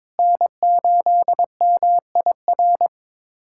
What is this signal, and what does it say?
Morse code, heard as a keyed tone, about 22 wpm, 700 Hz. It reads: N8MIR
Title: USA Callsign DE Callsign - 5 Characters - 22wpm